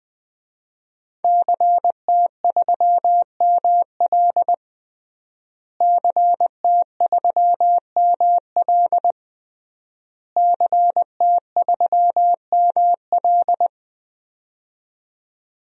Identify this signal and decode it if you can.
Morse code, heard as a keyed tone, about 20 wpm, 700 Hz. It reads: CT3ML CT3ML CT3ML
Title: DXCC-100 Call Signs - 5 Characters - Repeated 3x - 20wpm